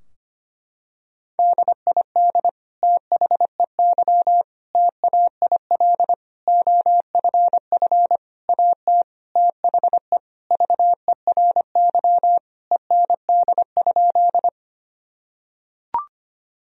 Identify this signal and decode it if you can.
Morse code, heard as a keyed tone, about 25 wpm, 700 Hz. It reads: DID THEY TAIL OFF AT THE VERY END?
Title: Sentences from Top 1000 Words - Review Part 3 - 25wpm